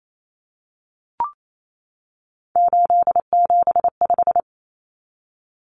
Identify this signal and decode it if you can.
Morse code, heard as a keyed tone, about 28 wpm, 700 Hz. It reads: 875
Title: Numbers - 3-Digit 28wpm